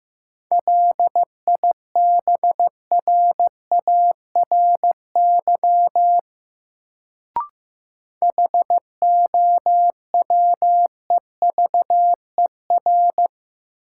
Morse code, heard as a keyed tone, about 15 wpm, 700 Hz. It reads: LIBRARY HOWEVER
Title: Sentences from Top 600 Words - New Word Drill - 15wpm